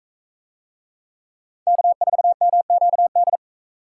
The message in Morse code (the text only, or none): K4MQD